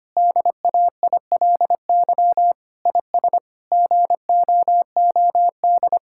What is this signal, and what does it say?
Morse code, heard as a keyed tone, about 25 wpm, 700 Hz. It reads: DAILY IS GOOD
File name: practice / 25wpm / Difficult Sentences with Hints - 25wpm